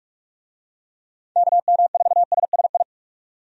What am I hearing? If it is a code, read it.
Morse code, heard as a keyed tone, about 46 wpm, 700 Hz. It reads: KM4SSI
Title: USA Call Signs - 6 Characters - 45wpm